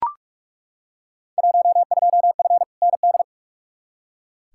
Morse code, heard as a keyed tone, about 41 wpm, 700 Hz. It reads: 12F ND
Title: Field Day Exchanges Encore - 45wpm